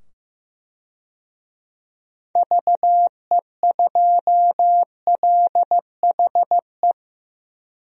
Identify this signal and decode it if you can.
Morse code, heard as a keyed tone, about 15 wpm, 700 Hz. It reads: VE2LHE
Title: DXCC-100 Call Signs - 6 Characters - 15wpm